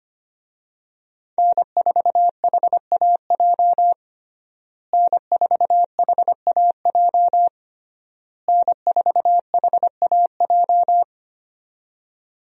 Morse code, heard as a keyed tone, about 25 wpm, 700 Hz. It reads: N4HAJ N4HAJ N4HAJ
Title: USA Call Signs - 5 Characters - Repeated 3x - 25wpm